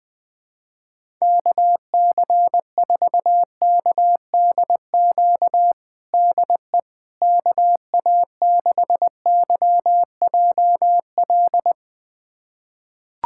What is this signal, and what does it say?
Morse code, heard as a keyed tone, about 20 wpm, 700 Hz. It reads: KC4KDQ DE KA6YJL
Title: USA Callsign DE Callsign - 6 Characters - 20wpm